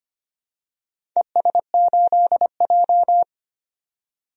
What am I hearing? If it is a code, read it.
Morse code, heard as a keyed tone, about 25 wpm, 700 Hz. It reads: ES8J
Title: DXCC-100 Call Signs - 4 Characters - 25wpm